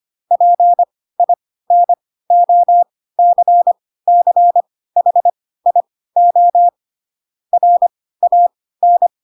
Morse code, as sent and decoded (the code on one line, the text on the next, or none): .--. .. -. --- -.-. -.-. .... .. --- / .-. .- -.
PINOCCHIO RAN